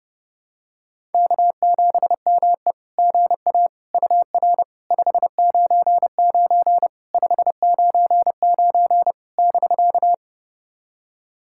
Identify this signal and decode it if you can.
Morse code, heard as a keyed tone, about 32 wpm, 700 Hz. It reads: K7ME GA UR 599 599 <BK>
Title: SOTA QSO Practice 30wpm